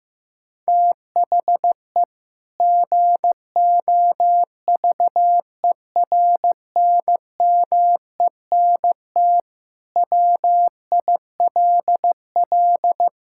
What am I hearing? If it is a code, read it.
Morse code, heard as a keyed tone, about 15 wpm, 700 Hz. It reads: THE GOVERNMENT WILL
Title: Sentences from Top 1000 Words - Review Part 5 - 15wpm